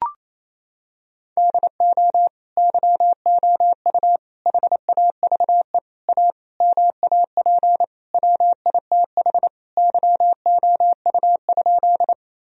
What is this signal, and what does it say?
Morse code, heard as a keyed tone, about 28 wpm, 700 Hz. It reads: DO YOU HAVE A MAP WITH YOU?
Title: Sentences from Top 1000 Words - Review Part 2 - 28wpm